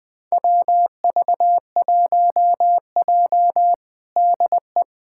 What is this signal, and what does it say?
Morse code, heard as a keyed tone, about 21 wpm, 700 Hz. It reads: WV1J DE
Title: USA Callsign DE Callsign - 4 Characters - 20wpm